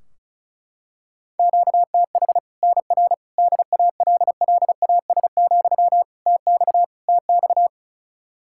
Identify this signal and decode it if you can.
Morse code, heard as a keyed tone, about 34 wpm, 700 Hz. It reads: QTH NR DALLAS, TX TX
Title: Practice QSO - Exchange 1 - Beginning - 35wpm